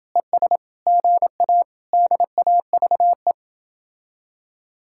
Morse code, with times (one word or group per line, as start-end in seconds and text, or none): 0.2-0.6 ES
0.9-1.6 GA
1.9-3.3 DAVE